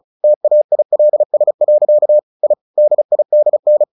Morse code, heard as a keyed tone, about 34 wpm, 600 Hz. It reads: TAILS. I DIDN
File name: the old man and the sea 2 35wpm